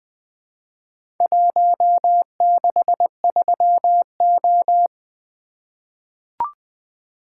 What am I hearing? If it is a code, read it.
Morse code, heard as a keyed tone, about 20 wpm, 700 Hz. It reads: 163O
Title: Four Letter-Number Combinations - 20wpm